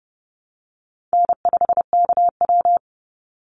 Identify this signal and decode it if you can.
Morse code, heard as a keyed tone, about 30 wpm, 700 Hz. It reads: N5KW
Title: Call Signs Repeated 3x - 30wpm